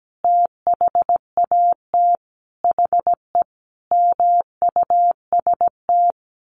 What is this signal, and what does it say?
Morse code, heard as a keyed tone, about 17 wpm, 700 Hz. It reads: THAT HE MUST
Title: Sentences from Top 1500 Words 17wpm